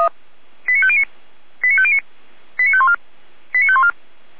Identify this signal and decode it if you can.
Five-tone selective call (ZVEI), heard as quick runs of five tones, tones of about 70 ms. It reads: 89509 89509 89514 89514